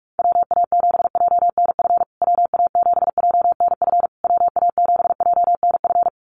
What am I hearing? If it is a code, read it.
Morse code, heard as a keyed tone, about 44 wpm, 700 Hz. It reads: WA7JNF WA7JNF WA7JNF
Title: USA Call Signs Repeated 3x 45wpm